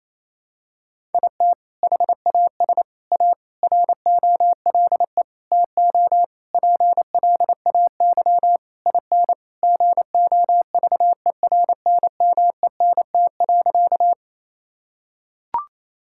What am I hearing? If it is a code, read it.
Morse code, heard as a keyed tone, about 28 wpm, 700 Hz. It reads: IT HAS A ROLE TO PLAY IN GOVERNMENT.
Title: Sentences from Top 1000 Words - Review Part 6 - 28wpm